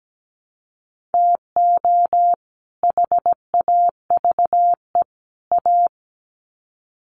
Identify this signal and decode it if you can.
Morse code, heard as a keyed tone, about 17 wpm, 700 Hz. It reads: TO HAVE A
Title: Sets of 3 Words 17wpm